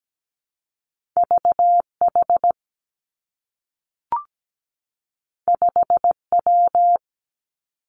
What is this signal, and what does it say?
Morse code, heard as a keyed tone, about 17 wpm, 700 Hz. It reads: VH 5W
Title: Two Challenging Letter-Number Combinations 17wpm